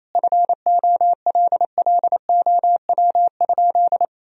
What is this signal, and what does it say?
Morse code, heard as a keyed tone, about 27 wpm, 700 Hz. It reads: FOLLOW?
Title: Sentences from Top 1000 Words - Review Part 2 - 28wpm